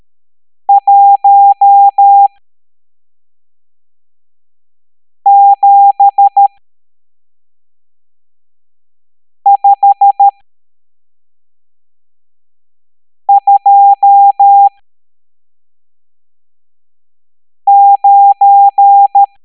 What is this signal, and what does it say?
Morse code, heard as a keyed tone, about 13 wpm, spaced as for 3 wpm, 800 Hz. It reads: 17529
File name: hamdata / L12-EX02-CHIFFRES